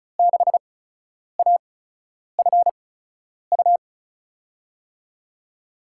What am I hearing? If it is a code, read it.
Morse code, heard as a keyed tone, about 34 wpm, spaced as for 10 wpm, 700 Hz. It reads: 6AFU
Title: USA Call Signs - 5 Characters - 8x Character Spacing - 35wpm